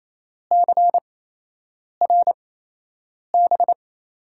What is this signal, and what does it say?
Morse code, heard as a keyed tone, about 28 wpm, spaced as for 8 wpm, 700 Hz. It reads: CRB
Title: USA Call Signs - 8x Character Spacing - 28wpm